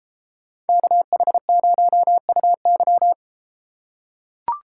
Morse code, heard as a keyed tone, about 30 wpm, 700 Hz. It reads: KH0UY